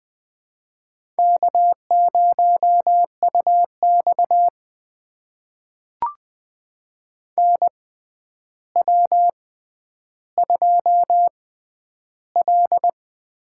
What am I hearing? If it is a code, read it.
Morse code, heard as a keyed tone, about 20 wpm, spaced as for 10 wpm, 700 Hz. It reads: K0UX NW2L
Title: USA Call Signs - 4 Characters - 6x Character Spacing - 20wpm